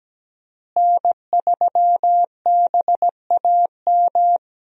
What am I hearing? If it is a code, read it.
Morse code, heard as a keyed tone, about 17 wpm, 700 Hz. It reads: N3BAM
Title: USA Call Signs - 5 Characters - Repeated 2x - 17wpm